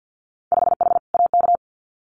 Morse code, heard as a keyed tone, about 50 wpm, 700 Hz. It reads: 5H AK
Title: Winter Field Day Exchanges Encore 50wpm